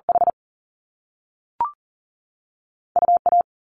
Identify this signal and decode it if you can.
Morse code, heard as a keyed tone, about 38 wpm, 700 Hz. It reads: H U A
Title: Top Two Letter Combinations in Words 40wpm